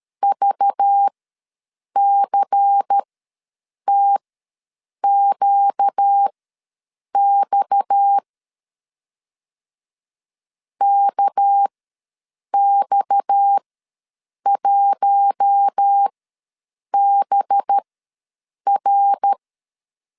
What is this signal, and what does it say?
Morse code, heard as a keyed tone, about 13 wpm, spaced as for 7 wpm, 800 Hz. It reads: VCTQX KX1BR
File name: LEZIONI / 6 PKX0 / 6 PKX ws 12wpm 3106